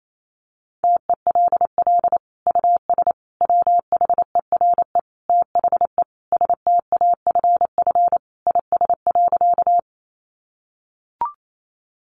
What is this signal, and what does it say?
Morse code, heard as a keyed tone, about 28 wpm, 700 Hz. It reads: TELL US WHERE THE STAFF IS.
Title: Sentences from Top 1200 Words 28wpm